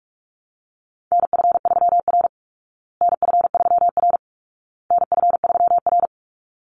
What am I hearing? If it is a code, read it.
Morse code, heard as a keyed tone, about 44 wpm, 700 Hz. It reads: NF3R NF3R NF3R
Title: Call Signs Repeated 2x - 45wpm